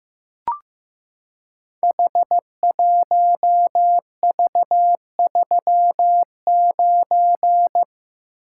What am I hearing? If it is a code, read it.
Morse code, heard as a keyed tone, about 15 wpm, 700 Hz. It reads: H1V39
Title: Code Groups - 5 Letter:Numbers - 15wpm